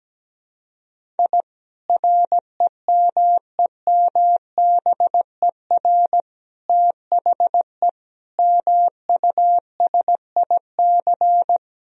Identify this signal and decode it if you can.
Morse code, heard as a keyed tone, about 17 wpm, 700 Hz. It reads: I REMEMBER THE MUSIC
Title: Sentences from Top 1000 Words - Review Part 1 - 17wpm